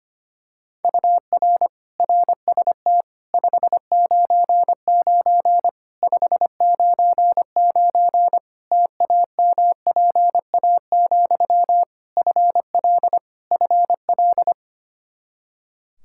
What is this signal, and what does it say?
Morse code, heard as a keyed tone, about 25 wpm, 700 Hz. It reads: UR RST 599 599 TAMPA, FL FL
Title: K1USN SST QSO Practice 25wpm